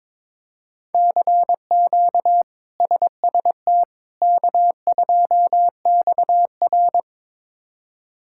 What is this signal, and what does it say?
Morse code, heard as a keyed tone, about 22 wpm, 700 Hz. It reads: CQ SST K2XR
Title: K1USN SST QSO Practice - 22wpm